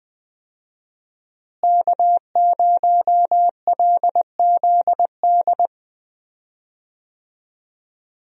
Morse code, heard as a keyed tone, about 20 wpm, 700 Hz. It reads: K0LZD
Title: USA Call Signs - 5 Characters - Repeated 3x - 20wpm